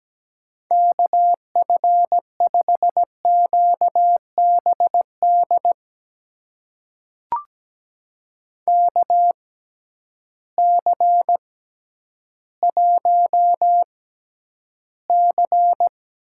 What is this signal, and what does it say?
Morse code, heard as a keyed tone, about 17 wpm, spaced as for 10 wpm, 700 Hz. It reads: KF5QBD KC1C
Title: USA Call Signs - 6x Character Spacing - 17wpm